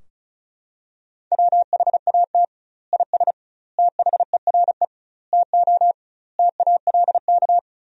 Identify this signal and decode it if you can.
Morse code, heard as a keyed tone, about 34 wpm, 700 Hz. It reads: WHAT IS THERE TO TALK
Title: 2x Word-Spacing - Sentences from Top 400 Words - 35wpm